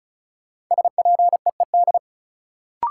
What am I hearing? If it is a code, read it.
Morse code, heard as a keyed tone, about 36 wpm, 700 Hz. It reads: SPEED
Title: Sentences from Top 800 Words - New Word Drill - 35wpm